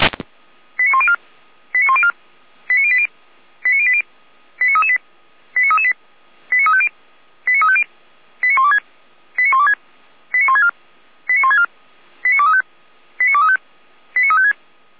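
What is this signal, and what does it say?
Five-tone selective call (ZVEI), heard as quick runs of five tones, tones of about 70 ms. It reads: 89104 89104 89080 89080 89338 89338 89350 89350 89126 89126 89164 89164 89235 89235 89356
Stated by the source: ZVEI five-tone selective call received by SDR.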